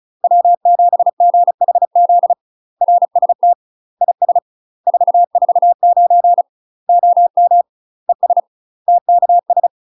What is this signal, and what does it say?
Morse code, heard as a keyed tone, about 36 wpm, 700 Hz. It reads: W7GHZ RST IS 449 OM ES TKS
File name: UK MASTER QSOs TRAINING / 35WPM / UK MASTER QSOs 0041